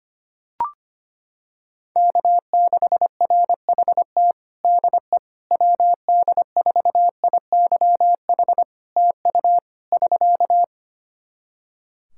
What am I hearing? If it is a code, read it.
Morse code, heard as a keyed tone, about 25 wpm, 700 Hz. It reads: K6RHT DE WD4IYH TU <SK>
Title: Practice QSO - Exchange 4 - Ending - 25wpm